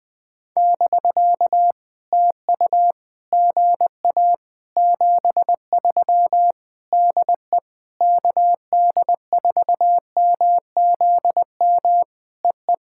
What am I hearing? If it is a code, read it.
Morse code, heard as a keyed tone, about 19 wpm, 700 Hz. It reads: <BK> TU GA 73 DE KD4MZM EE
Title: POTA QSO Practice - 20wpm